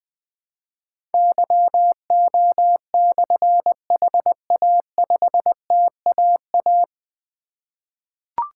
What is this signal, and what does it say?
Morse code, heard as a keyed tone, about 20 wpm, 700 Hz. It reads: YO/HA5TAA
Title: Difficult Call Signs 20wpm